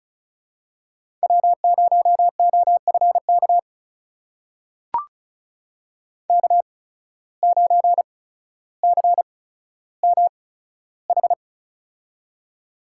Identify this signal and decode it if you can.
Morse code, heard as a keyed tone, about 36 wpm, spaced as for 15 wpm, 700 Hz. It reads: W0OFK K9CMH
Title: USA Call Signs - 5 Characters - 8x Character Spacing - 35wpm